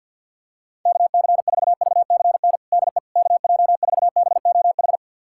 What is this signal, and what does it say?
Morse code, heard as a keyed tone, about 51 wpm, 690 Hz. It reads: KC4VCN DE KJ4BOH
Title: USA Callsign DE Callsign - 6 Characters - 50wpm